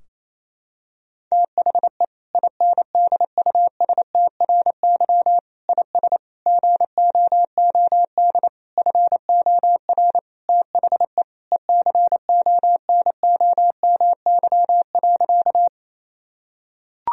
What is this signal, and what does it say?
Morse code, heard as a keyed tone, about 28 wpm, 700 Hz. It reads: THE INDUSTRY IS GOOD FOR THE ECONOMY.
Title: Sentences from Top 1000 Words - Review Part 4 - 28wpm